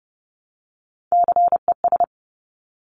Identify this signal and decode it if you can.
Morse code, heard as a keyed tone, about 30 wpm, 700 Hz. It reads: CES
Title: Top Three Letter Combinations in Words 30wpm